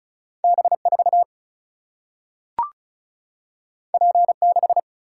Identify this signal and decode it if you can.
Morse code, heard as a keyed tone, about 36 wpm, 700 Hz. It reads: B4 P6
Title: Single-Letter Single-Number 35wpm